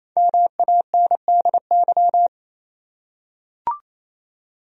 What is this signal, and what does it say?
Morse code, heard as a keyed tone, about 27 wpm, 700 Hz. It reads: MANDY
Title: USA Common Names - Max 5 Letters - 28wpm